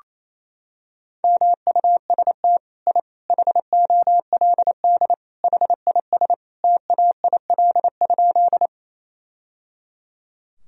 Morse code, heard as a keyed tone, about 27 wpm, 700 Hz. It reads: MUST I HOLD HIS TAIL?
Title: Sentences from Top 1000 Words - Review Part 3 - 28wpm